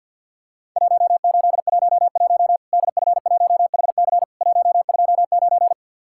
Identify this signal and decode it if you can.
Morse code, heard as a keyed tone, about 48 wpm, 700 Hz. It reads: 1811 DF1HC 129
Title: Worked All Europe DX Contest - QTC Exchange - 50wpm